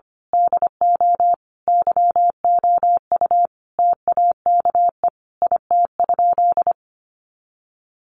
Morse code, heard as a keyed tone, about 25 wpm, 700 Hz. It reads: DO YOU TAKE IT?
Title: Sentences from Top 200 Words 25wpm